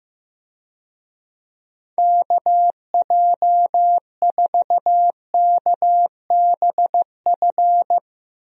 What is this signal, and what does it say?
Morse code, heard as a keyed tone, about 15 wpm, 700 Hz. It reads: KJ4KBF
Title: USA Call Signs - 6 Characters - 15wpm